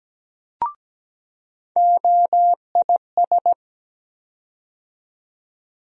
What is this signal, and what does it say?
Morse code, heard as a keyed tone, about 17 wpm, 700 Hz. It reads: OIS
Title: USA Call Sign Postfixes - 3 Letters 17wpm